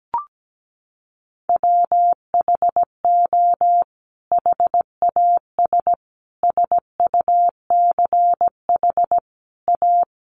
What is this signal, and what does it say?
Morse code, heard as a keyed tone, about 17 wpm, 700 Hz. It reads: WHO HAS SUCH A
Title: Sentences from Top 1200 Words 17wpm